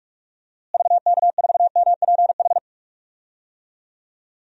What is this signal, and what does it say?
Morse code, heard as a keyed tone, about 41 wpm, 700 Hz. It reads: VK4MPH
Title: DXCC-100 Call Signs - 6 Characters - Repeated 3x - 45wpm